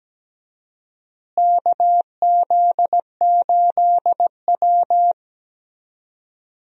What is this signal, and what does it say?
Morse code, heard as a keyed tone, about 17 wpm, 700 Hz. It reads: KZ8W
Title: USA Call Signs - 4 Characters - Repeated 3x - 17wpm